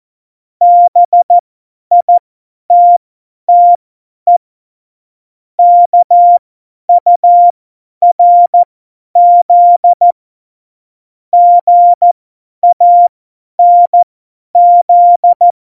Morse code, heard as a keyed tone, about 14 wpm, spaced as for 10 wpm, 700 Hz. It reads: BITTE KURZ GANZ
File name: Morsekurs / Witze / Witz 08. September 2020, 50 F70